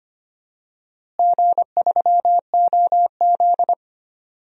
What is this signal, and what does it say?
Morse code, heard as a keyed tone, about 25 wpm, 700 Hz. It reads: G3OZ
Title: DXCC-100 Call Signs - 4 Characters - Repeated 2x - 25wpm